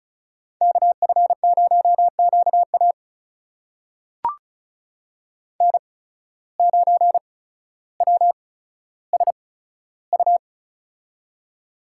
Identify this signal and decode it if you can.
Morse code, heard as a keyed tone, about 36 wpm, spaced as for 15 wpm, 700 Hz. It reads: KF0QA N9WSU
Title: USA Call Signs - 5 Characters - 8x Character Spacing - 35wpm